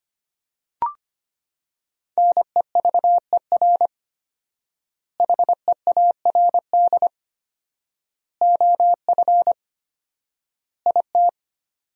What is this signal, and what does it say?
Morse code, heard as a keyed tone, about 25 wpm, 700 Hz. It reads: NEVER HEARD OF IT
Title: 4x Word-Spacing - Sentences from Top 400 Words - 25wpm